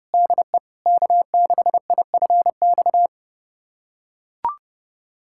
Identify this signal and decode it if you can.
Morse code, heard as a keyed tone, about 32 wpm, 700 Hz. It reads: DE K6IFX
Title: USA Callsign DE Callsign - 5 Characters - 30wpm